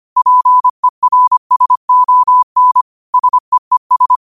Morse code, heard as a keyed tone, about 25 wpm, 1000 Hz. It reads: PERSON SEES